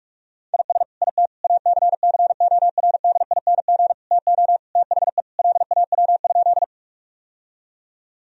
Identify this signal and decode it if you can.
Morse code, heard as a keyed tone, about 43 wpm, 700 Hz. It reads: IS IT ACCORDING TO THE LAW?
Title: Sentences from Top 1000 Words - Review Part 4 - 45wpm